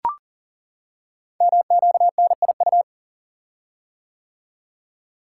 Morse code, heard as a keyed tone, about 39 wpm, 700 Hz. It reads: MQNIU
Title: Code Groups - 5 Letters - 40wpm